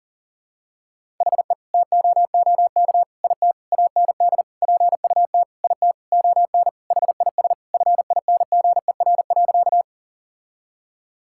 Morse code, heard as a keyed tone, about 37 wpm, 700 Hz. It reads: HE TOOK IT AND PUT IT ON HIS FINGER.
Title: Sentences from Top 1600 Words 40wpm